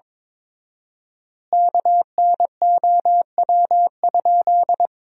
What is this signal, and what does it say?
Morse code, heard as a keyed tone, about 22 wpm, 700 Hz. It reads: KNOW?
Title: 4x Word-Spacing - Sentences from Top 100 Words - 22wpm